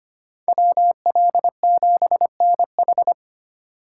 Morse code, heard as a keyed tone, about 33 wpm, 700 Hz. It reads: WL7NH